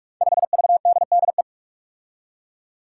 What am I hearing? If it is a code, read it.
Morse code, heard as a keyed tone, about 46 wpm, 700 Hz. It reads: 5VDDE
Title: Five Challenging Letter-Number Combinations 45wpm